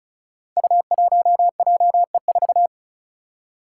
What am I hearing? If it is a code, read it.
Morse code, heard as a keyed tone, about 34 wpm, 700 Hz. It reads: U1JE4
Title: Five Challenging Letter-Number Combinations 35wpm